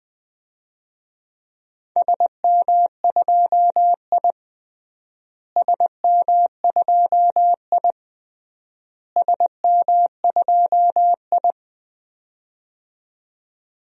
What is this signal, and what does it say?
Morse code, heard as a keyed tone, about 20 wpm, 700 Hz. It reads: SM2I SM2I SM2I
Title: DXCC-100 Call Signs - 4 Characters - Repeated 3x - 20wpm